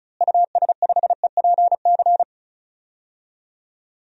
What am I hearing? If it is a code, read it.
Morse code, heard as a keyed tone, about 36 wpm, 700 Hz. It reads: US5EPC
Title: DXCC-100 Call Signs - 6 Characters - Repeated 3x - 35wpm